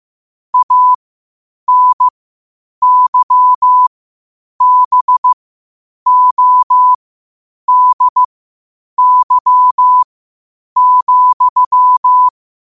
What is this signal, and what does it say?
Morse code, heard as a keyed tone, about 15 wpm, spaced as for 8 wpm, 1000 Hz. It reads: ANYBODY,